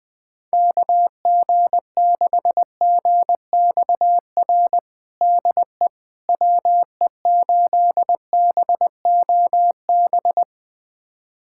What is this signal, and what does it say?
Morse code, heard as a keyed tone, about 19 wpm, 700 Hz. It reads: KG6GXR DE WE8BOB
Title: USA Callsign DE Callsign 20wpm